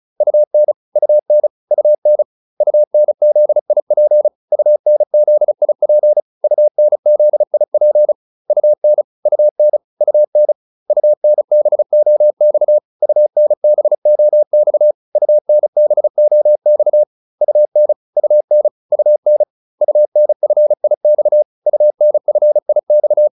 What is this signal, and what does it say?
Morse code, heard as a keyed tone, about 34 wpm, 600 Hz. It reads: UN UN UN UNZIP UNZIP UNZIP UN UN UN UNBOX UNBOX UNBOX UN UN UN UNFIX UNFIX
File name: UN5 35